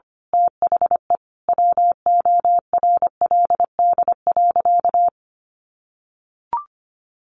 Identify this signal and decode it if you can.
Morse code, heard as a keyed tone, about 25 wpm, 700 Hz. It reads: THE WORLD.
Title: Sentences from Top 1400 Words 25wpm